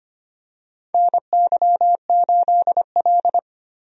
Morse code, heard as a keyed tone, about 25 wpm, 700 Hz. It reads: NY8L
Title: USA Call Signs - 4 Characters - Repeated 2x - 25wpm